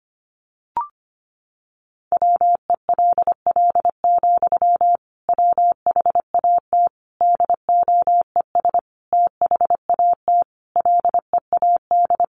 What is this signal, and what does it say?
Morse code, heard as a keyed tone, about 25 wpm, 700 Hz. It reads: WELL, WHAT DOES THAT LEAD
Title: Sentences from Top 500 Words 25wpm